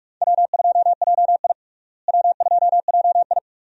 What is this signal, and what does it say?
Morse code, heard as a keyed tone, about 46 wpm, 700 Hz. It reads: W2JI W2JI
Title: USA Call Signs - 4 Characters - Repeated 2x - 45wpm